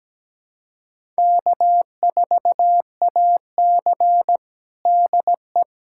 Morse code, heard as a keyed tone, about 17 wpm, 700 Hz. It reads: K4AC DE
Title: USA Callsign DE Callsign - 4 Characters - 17wpm